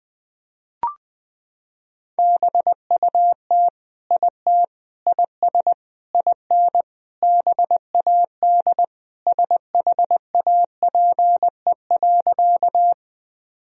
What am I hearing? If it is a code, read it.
Morse code, heard as a keyed tone, about 20 wpm, 700 Hz. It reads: BUT IT IS IN BAD SHAPE.
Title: Sentences from Top 1000 Words - Review Part 3 - 20wpm